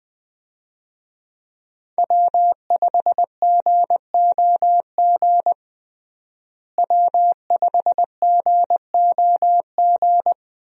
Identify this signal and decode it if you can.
Morse code, heard as a keyed tone, about 20 wpm, 700 Hz. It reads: W5GOG W5GOG
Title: USA Call Signs - 5 Characters - Repeated 2x - 20wpm